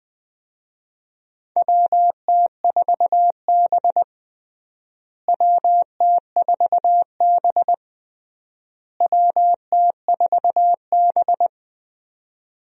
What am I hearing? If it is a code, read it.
Morse code, heard as a keyed tone, about 20 wpm, 700 Hz. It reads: WT4B WT4B WT4B
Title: USA Call Signs - 4 Characters - Repeated 3x - 20wpm